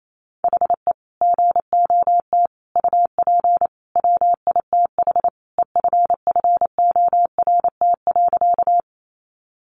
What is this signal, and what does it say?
Morse code, heard as a keyed tone, about 28 wpm, 700 Hz. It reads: HE GOT UP WITH EFFORT.
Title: Sentences from Top 1200 Words 28wpm